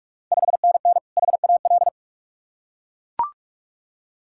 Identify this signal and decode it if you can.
Morse code, heard as a keyed tone, about 44 wpm, 690 Hz. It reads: 5NN HAL
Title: Ontario QSO Party Exchange 45wpm